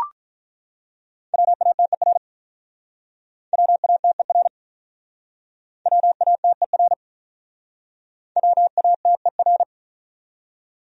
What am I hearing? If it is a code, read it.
Morse code, heard as a keyed tone, about 47 wpm, 700 Hz. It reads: WATER WATER WATER WATER